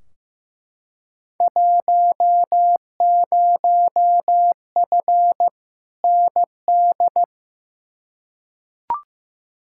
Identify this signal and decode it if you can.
Morse code, heard as a keyed tone, about 15 wpm, 700 Hz. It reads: 10F ND
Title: Field Day Exchanges Encore - 15wpm